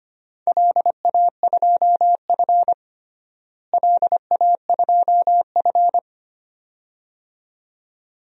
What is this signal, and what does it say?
Morse code, heard as a keyed tone, about 25 wpm, 700 Hz. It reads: LA2F LA2F
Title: DXCC-100 Call Signs - 4 Characters - Repeated 2x - 25wpm